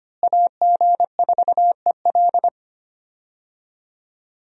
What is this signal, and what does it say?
Morse code, heard as a keyed tone, about 25 wpm, 700 Hz. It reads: AG4EL
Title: USA Call Signs - 5 Characters - Repeated 3x - 25wpm